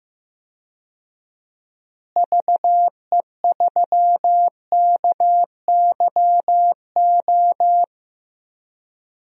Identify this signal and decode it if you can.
Morse code, heard as a keyed tone, about 15 wpm, 700 Hz. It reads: VE3KYO
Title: DXCC-100 Call Signs - 6 Characters - Repeated 2x - 15wpm